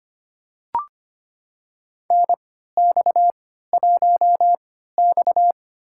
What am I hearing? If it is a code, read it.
Morse code, heard as a keyed tone, about 25 wpm, spaced as for 14 wpm, 700 Hz. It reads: NX1X
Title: USA Call Signs - 4 Characters - 3x Character Spacing - 25wpm